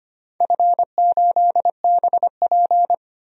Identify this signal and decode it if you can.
Morse code, heard as a keyed tone, about 25 wpm, 700 Hz. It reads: F8BP
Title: DXCC-100 Call Signs - 4 Characters - 25wpm